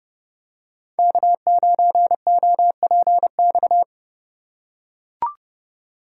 Morse code, heard as a keyed tone, about 30 wpm, 700 Hz. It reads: K9OPX